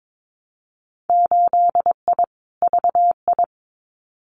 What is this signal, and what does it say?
Morse code, heard as a keyed tone, about 22 wpm, 700 Hz. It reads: 8I VI
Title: Winter Field Day Exchanges Encore 22wpm